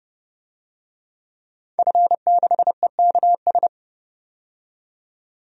Morse code, heard as a keyed tone, about 30 wpm, 700 Hz. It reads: F6EKS
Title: DXCC-100 Call Signs - 5 Characters - 30wpm